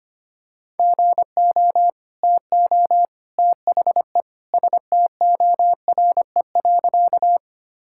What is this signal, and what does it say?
Morse code, heard as a keyed tone, about 25 wpm, 700 Hz. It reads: GO TO THE STORE.
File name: practice / 25wpm / Sentences with 2-Word Phrases - 25wpm